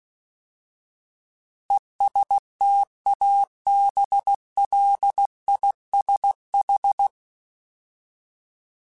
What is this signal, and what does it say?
Morse code, heard as a keyed tone, about 16 wpm, 800 Hz. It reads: ESTABLISH
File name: Alphabetical / Alphabetical mp3s / Es @15